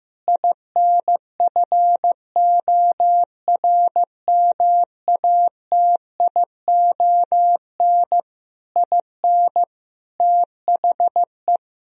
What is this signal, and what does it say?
Morse code, heard as a keyed tone, about 15 wpm, 700 Hz. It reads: INFORMATION IN THE
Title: Sentences from Top 1000 Words - Review Part 4 - 15wpm